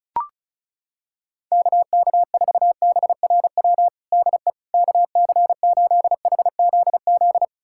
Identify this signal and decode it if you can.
Morse code, heard as a keyed tone, about 36 wpm, 700 Hz. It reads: KK4BRW DE KC8HZZ
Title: USA Callsign DE Callsign - 6 Characters - 35wpm